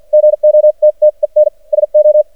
Morse code, heard as a keyed tone, about 53 wpm, spaced as for 42 wpm, 610 Hz. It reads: GOTTEN SO
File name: archived / CRSnet